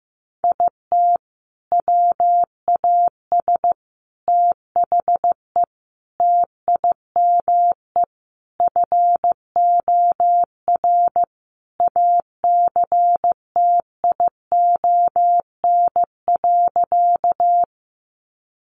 Morse code, heard as a keyed tone, about 15 wpm, 700 Hz. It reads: IT WAS THE TIME FOR ACTION.
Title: Sentences from Top 700 Words 15wpm